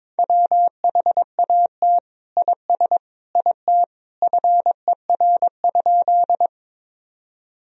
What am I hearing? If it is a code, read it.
Morse code, heard as a keyed tone, about 22 wpm, 700 Hz. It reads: WHAT IS IT FER?
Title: Sentences from Top 100 Words and Abbreviations 22wpm